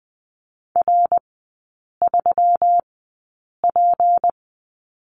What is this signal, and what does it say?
Morse code, heard as a keyed tone, about 20 wpm, spaced as for 8 wpm, 700 Hz. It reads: R3P
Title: 152 Lesson 25e - Letter p - Three Characters 20wpm